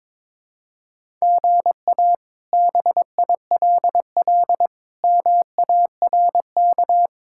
Morse code, heard as a keyed tone, about 22 wpm, 700 Hz. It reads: GA BILL MARK
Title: K1USN SST QSO Practice - 22wpm